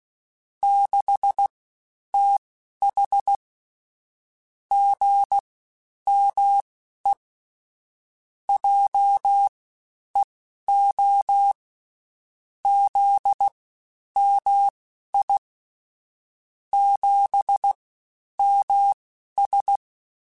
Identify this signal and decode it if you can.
Morse code, heard as a keyed tone, about 16 wpm, spaced as for 10 wpm, 800 Hz. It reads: 6TH GME JEO ZMI 7MS
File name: CW Comparisons / CW Comparisons-mp3s / CW Comparisons Broken Characters @15